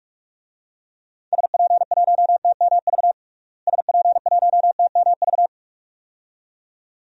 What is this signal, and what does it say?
Morse code, heard as a keyed tone, about 47 wpm, 700 Hz. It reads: SP1TMV SP1TMV
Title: DXCC-100 Call Signs - 6 Characters - Repeated 2x - 45wpm